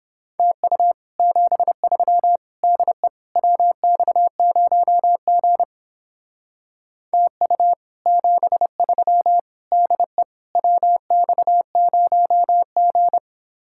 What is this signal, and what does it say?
Morse code, heard as a keyed tone, about 29 wpm, 700 Hz. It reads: TU 73 DE WX0G TU 73 DE WX0G